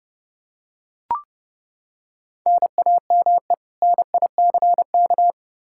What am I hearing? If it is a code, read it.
Morse code, heard as a keyed tone, about 32 wpm, 700 Hz. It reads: NAME NICK
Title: Practice QSO - Exchange 1 - Beginning - 30wpm